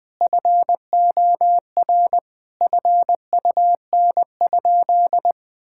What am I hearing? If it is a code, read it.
Morse code, heard as a keyed tone, about 20 wpm, 700 Hz. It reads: FOR FUN?
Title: Sentences with 4-Word Phrases - Spoken Beginning - 20wpm